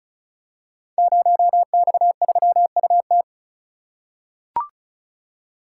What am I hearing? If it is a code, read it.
Morse code, heard as a keyed tone, about 34 wpm, 700 Hz. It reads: 0X3UT
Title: Code Groups - 5 Letter:Numbers - 35wpm